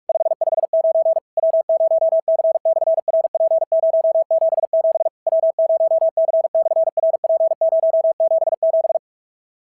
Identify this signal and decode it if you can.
Morse code, heard as a keyed tone, about 44 wpm, 660 Hz. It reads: 559 W0C/RP077 W0C/RP077
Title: SOTA QSO Practice 45wpm